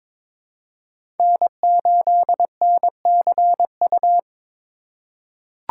Morse code, heard as a keyed tone, about 22 wpm, 700 Hz. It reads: N8NCU
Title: USA Call Signs - 5 Characters - 4x Character Spacing - 22wpm